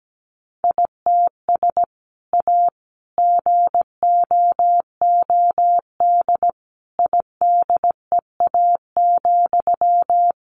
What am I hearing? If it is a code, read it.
Morse code, heard as a keyed tone, about 17 wpm, 700 Hz. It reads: ITS A GOOD IDEA,
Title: Sentences from Top 600 Words 17wpm